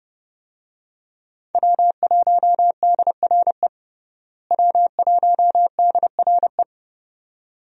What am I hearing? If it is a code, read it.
Morse code, heard as a keyed tone, about 30 wpm, 700 Hz. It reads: W1DRE W1DRE
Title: USA Call Signs - 5 Characters - Repeated 2x - 30wpm